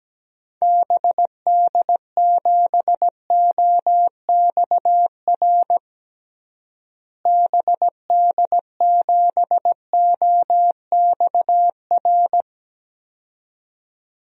Morse code, heard as a keyed tone, about 17 wpm, 700 Hz. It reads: BD7OXR BD7OXR
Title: DXCC-100 Call Signs - 6 Characters - Repeated 2x - 17wpm